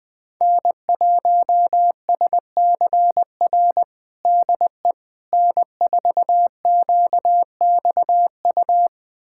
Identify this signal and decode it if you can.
Morse code, heard as a keyed tone, about 21 wpm, 700 Hz. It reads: N1SCR DE N4QXU
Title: USA Callsign DE Callsign - 5 Characters - 20wpm